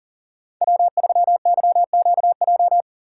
Morse code, heard as a keyed tone, about 39 wpm, 700 Hz. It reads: W3YQJ
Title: Code Groups - 5 Letter:Numbers - 40wpm